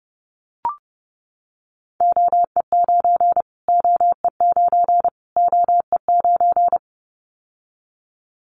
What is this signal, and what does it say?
Morse code, heard as a keyed tone, about 30 wpm, 700 Hz. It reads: OE9 OE9 OE9
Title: Callsign Prefixes for Top 25 DXCC - Repeated 3x - 30wpm